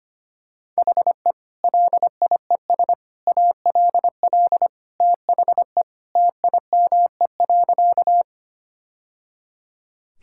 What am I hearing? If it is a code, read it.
Morse code, heard as a keyed tone, about 25 wpm, 700 Hz. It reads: HE LIES ALL THE TIME.
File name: practice / 25wpm / Sentences with 3-Word Phrases - 25wpm